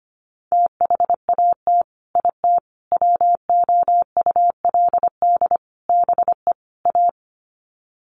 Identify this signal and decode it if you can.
Morse code, heard as a keyed tone, about 25 wpm, 700 Hz. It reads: THAT IT WOULD BE A
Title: Sets of 5 Words Encore2 25wpm